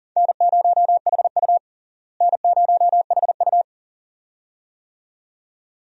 Morse code, heard as a keyed tone, about 37 wpm, 700 Hz. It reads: N0HU N0HU
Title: USA Call Signs - 4 Characters - Repeated 2x - 40wpm